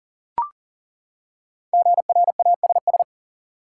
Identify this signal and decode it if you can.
Morse code, heard as a keyed tone, about 38 wpm, 700 Hz. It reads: GRASS
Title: Five Letter Words - 40wpm